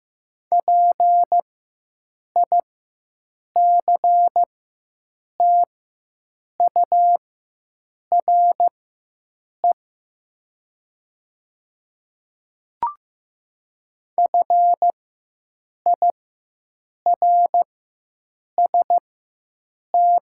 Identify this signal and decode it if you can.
Morse code, heard as a keyed tone, about 15 wpm, spaced as for 7 wpm, 700 Hz. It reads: PICTURE FIRST
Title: 4x Character Spacing - Top 200 Words - 15wpm